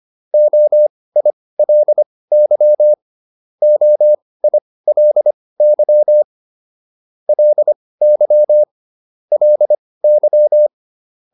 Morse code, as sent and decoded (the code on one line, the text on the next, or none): --- .. .-.. -.-- / --- .. .-.. -.-- / .-.. -.-- / .-.. -.--
OILY OILY LY LY